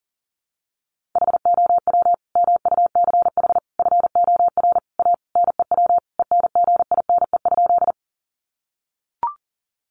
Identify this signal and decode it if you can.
Morse code, heard as a keyed tone, about 40 wpm, 700 Hz. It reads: HOW MUCH FOR A NEW ENGINE?
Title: Sentences from Top 1500 Words 40wpm